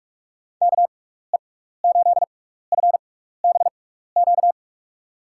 Morse code, heard as a keyed tone, about 41 wpm, spaced as for 15 wpm, 700 Hz. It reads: KE8FBQ
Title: USA Call Signs - 6x Character Spacing - 45wpm